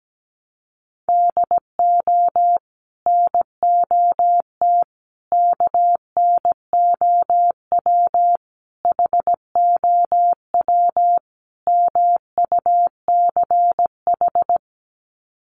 Morse code, heard as a keyed tone, about 17 wpm, 700 Hz. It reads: DO NOT KNOW HOW MUCH
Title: Sets of 5 Words Encore 17wpm